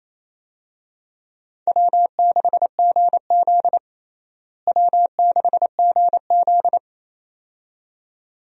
Morse code, heard as a keyed tone, about 28 wpm, 700 Hz. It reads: W6GZ W6GZ
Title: USA Call Signs - 4 Characters - Repeated 2x - 28wpm